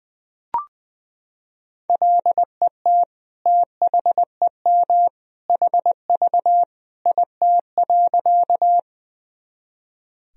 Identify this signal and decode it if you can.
Morse code, heard as a keyed tone, about 21 wpm, 700 Hz. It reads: LET THEM HV IT.
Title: Sentences from Top 300 Words and Abbreviations 20wpm